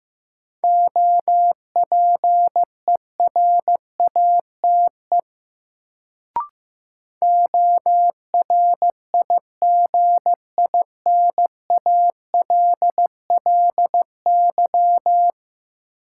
Morse code, heard as a keyed tone, about 16 wpm, 700 Hz. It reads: OPERATE ORIGINALLY
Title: Sentences from Top 1500 Words - New Word Drill Alt - 15wpm